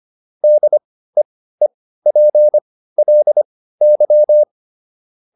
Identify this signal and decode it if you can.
Morse code, heard as a keyed tone, about 25 wpm, spaced as for 15 wpm, 600 Hz. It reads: DEEPLY